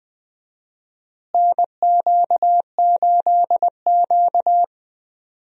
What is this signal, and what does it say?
Morse code, heard as a keyed tone, about 20 wpm, 700 Hz. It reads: NQ8Q
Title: USA Call Signs - 4 Characters - Repeated 3x - 20wpm